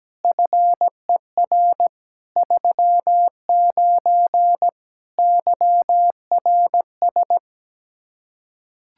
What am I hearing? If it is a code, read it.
Morse code, heard as a keyed tone, about 17 wpm, 700 Hz. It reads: FER 39 YRS
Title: K1USN SST QSO Practice 17wpm